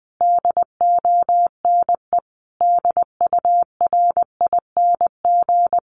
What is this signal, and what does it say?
Morse code, heard as a keyed tone, about 20 wpm, 700 Hz. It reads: DONE DURING
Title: Sentences from Top 1300 Words 20wpm